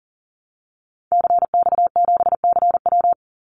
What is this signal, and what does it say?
Morse code, heard as a keyed tone, about 40 wpm, 700 Hz. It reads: CX7CW
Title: Call Signs Repeated 3x - 40wpm